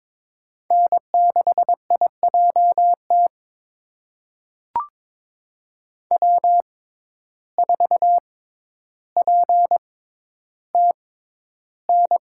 Morse code, heard as a keyed tone, about 22 wpm, spaced as for 11 wpm, 700 Hz. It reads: N6IJT W4PTN
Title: USA Call Signs - 5 Characters - 6x Character Spacing - 22wpm